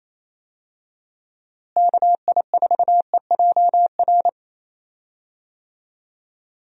Morse code, heard as a keyed tone, about 28 wpm, 700 Hz. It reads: KI4EJR
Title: USA Call Signs - 6 Characters - Repeated 3x - 28wpm